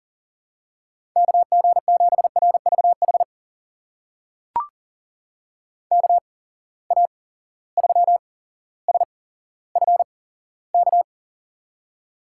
Morse code, heard as a keyed tone, about 40 wpm, spaced as for 17 wpm, 700 Hz. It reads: KG7RVH KA3SFK
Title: USA Call Signs - 6 Characters - 8x Character Spacing - 40wpm